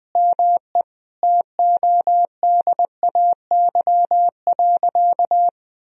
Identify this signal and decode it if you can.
Morse code, heard as a keyed tone, about 20 wpm, 700 Hz. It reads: ME TODAY.
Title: Sentences from Top 1000 Words - Review Part 4 - 20wpm